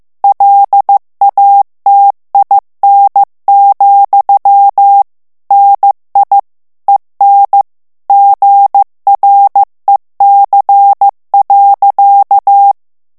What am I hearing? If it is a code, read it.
Morse code, heard as a keyed tone, about 15 wpm, 800 Hz. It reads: LATIN, NI EN GREC.
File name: FRANCAIS TEXTE CLAIR 2 VITESSE 14 MOTS MINUTES